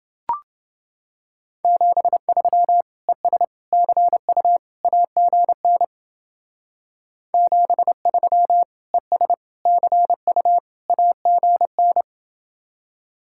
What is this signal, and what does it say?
Morse code, heard as a keyed tone, about 29 wpm, 700 Hz. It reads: 73 ES CU AGN 73 ES CU AGN